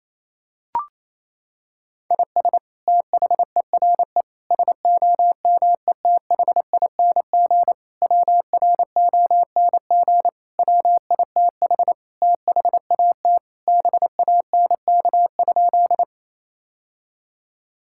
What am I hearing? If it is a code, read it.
Morse code, heard as a keyed tone, about 28 wpm, 700 Hz. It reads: IS THERE SOMETHING WRONG WITH THAT BANK?
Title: Sentences from Top 1000 Words - Review Part 6 - 28wpm